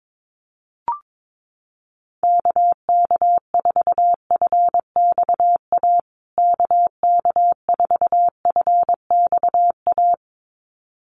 Morse code, heard as a keyed tone, about 22 wpm, 700 Hz. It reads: KK4FXA KK4FXA
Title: USA Call Signs Repeated 2x 22wpm